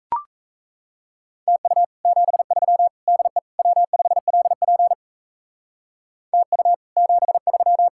Morse code, heard as a keyed tone, about 37 wpm, 700 Hz. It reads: TU 73 DE W5LP TU 73